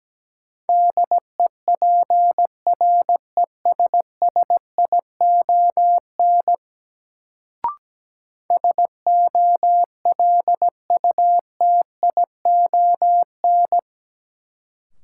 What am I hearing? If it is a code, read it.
Morse code, heard as a keyed tone, about 17 wpm, 700 Hz. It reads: DEPRESSION SOLUTION
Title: Sentences from Top 1300 Words - New Word Drill Alt - 17wpm